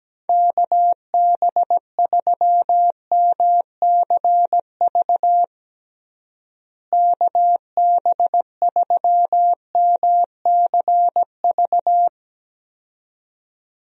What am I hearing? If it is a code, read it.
Morse code, heard as a keyed tone, about 17 wpm, 700 Hz. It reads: KB3MCV KB3MCV
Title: USA Call Signs - 6 Characters - Repeated 2x - 17wpm